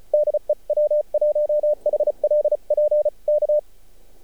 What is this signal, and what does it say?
Morse code, heard as a keyed tone, about 37 wpm, 600 Hz. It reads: DE W1HLP K
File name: archived / CRSnet